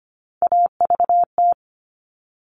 Morse code, heard as a keyed tone, about 25 wpm, 700 Hz. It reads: AVT
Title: All Cut Numbers - 3-Digit 25wpm